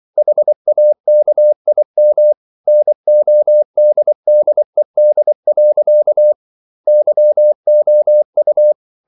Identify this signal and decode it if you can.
Morse code, heard as a keyed tone, about 24 wpm, 600 Hz. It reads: HAKIM NODDED. YOU